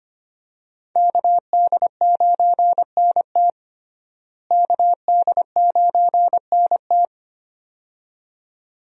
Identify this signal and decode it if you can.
Morse code, heard as a keyed tone, about 25 wpm, 700 Hz. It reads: KD9NT KD9NT
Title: USA Call Signs - 5 Characters - Repeated 2x - 25wpm